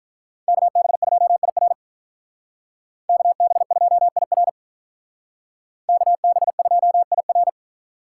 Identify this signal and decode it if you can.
Morse code, heard as a keyed tone, about 45 wpm, 700 Hz. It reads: KB2IR KB2IR KB2IR